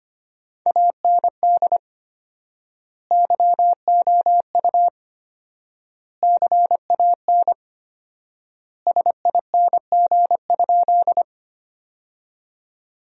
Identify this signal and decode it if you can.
Morse code, heard as a keyed tone, about 25 wpm, 700 Hz. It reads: AND YOU CAN SING?
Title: 4x Word-Spacing - Sentences from Top 400 Words - 25wpm